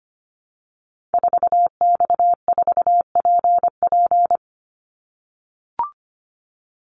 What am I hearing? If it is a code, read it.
Morse code, heard as a keyed tone, about 25 wpm, 700 Hz. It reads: 4X4PP
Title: Call Signs - 25wpm